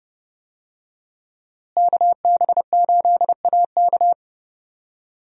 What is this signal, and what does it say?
Morse code, heard as a keyed tone, about 30 wpm, 700 Hz. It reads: KB8AK
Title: USA Call Signs - 5 Characters - Repeated 3x - 30wpm